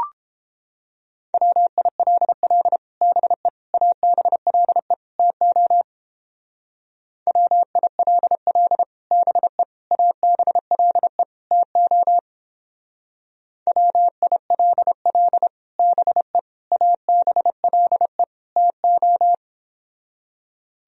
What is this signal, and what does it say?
Morse code, heard as a keyed tone, about 29 wpm, 700 Hz. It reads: WILL BE ABLE TO WILL BE ABLE TO WILL BE ABLE TO